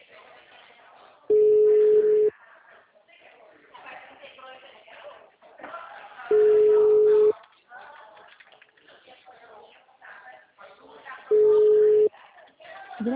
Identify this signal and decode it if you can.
Ringback tone (Europe, 425 Hz), with speech.
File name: out- -2004- - - .11971